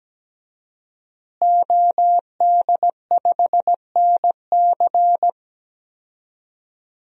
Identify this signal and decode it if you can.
Morse code, heard as a keyed tone, about 17 wpm, 700 Hz. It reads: OD5NC
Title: DXCC-100 Call Signs - 5 Characters - Repeated 2x - 17wpm